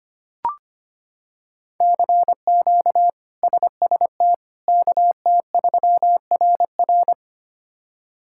Morse code, heard as a keyed tone, about 25 wpm, 700 Hz. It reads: CQ SST KT3RR
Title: K1USN SST QSO Practice - 25wpm